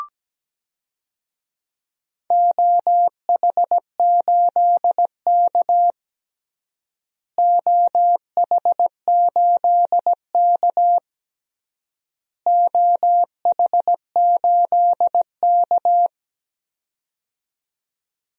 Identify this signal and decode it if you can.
Morse code, heard as a keyed tone, about 17 wpm, 700 Hz. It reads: OH8K OH8K OH8K
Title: DXCC-100 Call Signs - 4 Characters - Repeated 3x - 17wpm